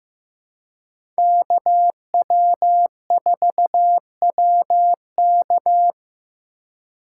Morse code, heard as a keyed tone, about 15 wpm, 700 Hz. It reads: KW4WK
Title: USA Call Signs - 5 Characters - 2x Character Spacing - 15wpm